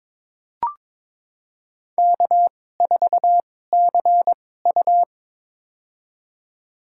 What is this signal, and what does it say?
Morse code, heard as a keyed tone, about 22 wpm, spaced as for 16 wpm, 700 Hz. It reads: K4CU
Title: USA Call Signs - 4 Characters - 2x Character Spacing - 22wpm